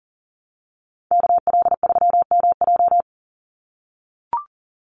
Morse code, heard as a keyed tone, about 40 wpm, 700 Hz. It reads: KL3MJ
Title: USA Call Signs 40wpm